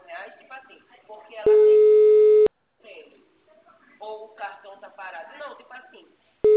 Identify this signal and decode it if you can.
Ringback tone (Europe, 425 Hz), with speech.